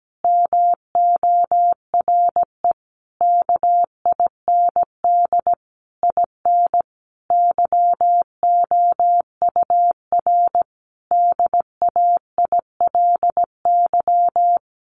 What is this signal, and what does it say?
Morse code, heard as a keyed tone, about 17 wpm, 700 Hz. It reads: MORE KIND IN YOUR DAILY
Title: Sentences from Top 1100 Words 17wpm